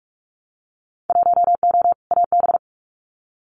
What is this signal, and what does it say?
Morse code, heard as a keyed tone, about 46 wpm, 700 Hz. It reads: 1O AB
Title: Winter Field Day Exchanges Encore 45wpm